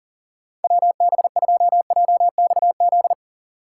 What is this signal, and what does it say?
Morse code, heard as a keyed tone, about 40 wpm, 700 Hz. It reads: WB2JXZ
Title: USA Call Signs - 6 Characters - 4x Character Spacing - 40wpm